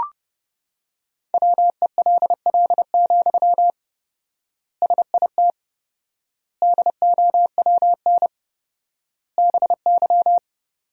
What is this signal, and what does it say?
Morse code, heard as a keyed tone, about 30 wpm, 700 Hz. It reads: WELL, SIT DOWN BY
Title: 4x Word-Spacing - Sentences from Top 500 Words - 30wpm